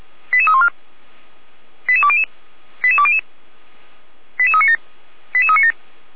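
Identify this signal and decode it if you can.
Five-tone selective call (ZVEI), heard as quick runs of five tones, tones of about 70 ms. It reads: 88314 88200 88200 88397 88397
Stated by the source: ZVEI five-tone selective call 88156, received on SDR.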